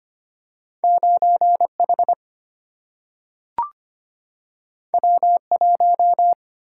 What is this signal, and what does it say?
Morse code, heard as a keyed tone, about 25 wpm, 700 Hz. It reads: 9 H W 1
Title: Single-Letter Single-Number 25wpm